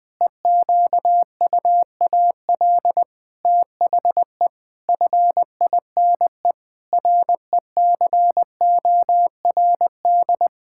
Morse code, heard as a keyed tone, about 21 wpm, 700 Hz. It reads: EQUAL THE FINE RECORD
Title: Pangram Challenge v2 - 20wpm